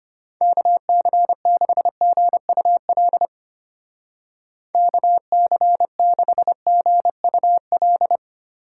Morse code, heard as a keyed tone, about 30 wpm, 700 Hz. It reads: KC6GUL KC6GUL
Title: Speed Racing - USA Call Signs 25wpm